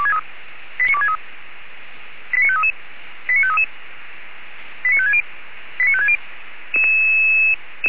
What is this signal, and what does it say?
Five-tone selective call (ZVEI), heard as quick runs of five tones, tones of about 70 ms. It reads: 89263 89533 89533 89560 89560